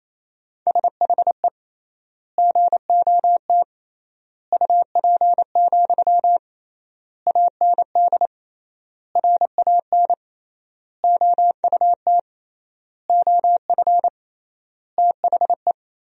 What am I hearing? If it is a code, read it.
Morse code, heard as a keyed tone, about 28 wpm, 700 Hz. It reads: SHE GOT UP, AND RAN OUT OF THE
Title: 3x Word-Spacing - Sentences from Top 500 Words - 28wpm